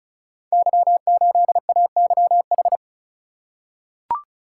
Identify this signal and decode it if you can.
Morse code, heard as a keyed tone, about 36 wpm, 700 Hz. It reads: Y8AYH
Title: Code Groups - 5 Letter:Numbers - 35wpm